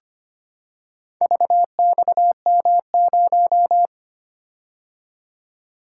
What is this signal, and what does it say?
Morse code, heard as a keyed tone, about 25 wpm, 700 Hz. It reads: VXM0
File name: practice / 25wpm / Four Letter-Number Combinations - 25wpm